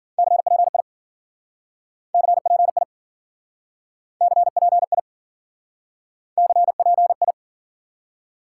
Morse code, heard as a keyed tone, about 51 wpm, 700 Hz. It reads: CPI CPI CPI CPI